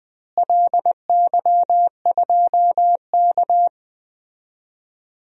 Morse code, heard as a keyed tone, about 20 wpm, 700 Hz. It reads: LY2K
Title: DXCC-100 Call Signs - 4 Characters - Repeated 3x - 20wpm